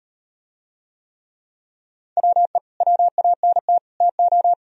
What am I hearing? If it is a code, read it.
Morse code, heard as a keyed tone, about 38 wpm, 700 Hz. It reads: WE WANT TO